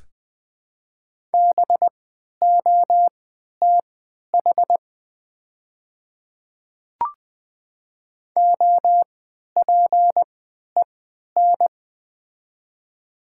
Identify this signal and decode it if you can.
Morse code, heard as a keyed tone, about 20 wpm, spaced as for 11 wpm, 700 Hz. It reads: BOTH OPEN
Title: 3x Character Spacing - Top 500 Words - 20wpm